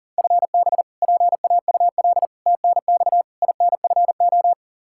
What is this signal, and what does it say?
Morse code, heard as a keyed tone, about 40 wpm, 700 Hz. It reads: FB PAUL TNX INFO
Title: Practice QSO - Exchange 3 - More Info - 40wpm